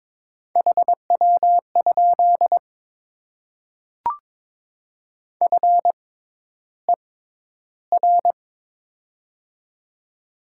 Morse code, heard as a keyed tone, about 22 wpm, spaced as for 11 wpm, 700 Hz. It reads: HW? FER
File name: Top 20 QSO Elements - 6x Character Spacing - 22wpm